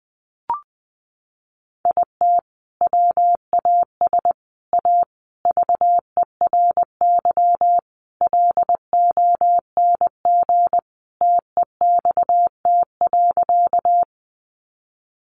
Morse code, heard as a keyed tone, about 20 wpm, 700 Hz. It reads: IT WAS A VERY LONG TEXT.
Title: Sentences from Top 1200 Words 20wpm